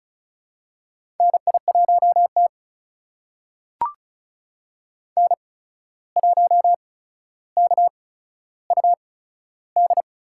Morse code, heard as a keyed tone, about 36 wpm, spaced as for 14 wpm, 700 Hz. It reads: NI1T N1KUD
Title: USA Call Signs - 8x Character Spacing - 35wpm